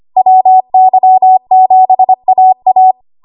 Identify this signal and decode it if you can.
Morse code, heard as a keyed tone, about 25 wpm, 750 Hz. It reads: WY7AA
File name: Morse / 25WPM